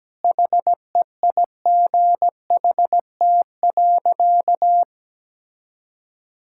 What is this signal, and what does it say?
Morse code, heard as a keyed tone, about 17 wpm, 700 Hz. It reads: HEIGHT.
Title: Sentences from Top 1600 Words 17wpm